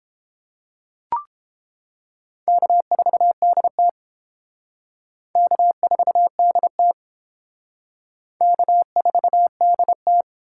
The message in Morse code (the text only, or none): K4DT K4DT K4DT